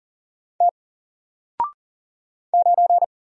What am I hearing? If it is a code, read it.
Morse code, heard as a keyed tone, about 40 wpm, 700 Hz. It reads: T 9
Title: Single Letter-Number Alt 40wpm